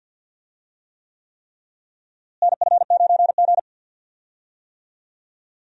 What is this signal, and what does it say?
Morse code, heard as a keyed tone, about 46 wpm, 700 Hz. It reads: NF9G
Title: USA Call Signs - 4 Characters - Repeated 3x - 50wpm